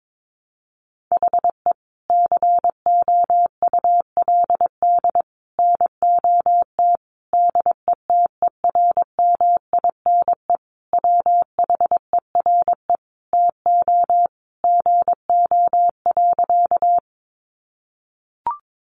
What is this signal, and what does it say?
Morse code, heard as a keyed tone, about 22 wpm, 700 Hz. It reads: HE COULD NOT DETERMINE WHERE TO GO.
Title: Sentences from Top 700 Words 22wpm